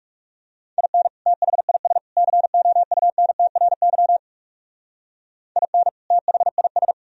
Morse code, heard as a keyed tone, about 34 wpm, 700 Hz. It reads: IN THIS COUNTRY IN THIS
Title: Speed-Racing - Sets of 3 Words 30wpm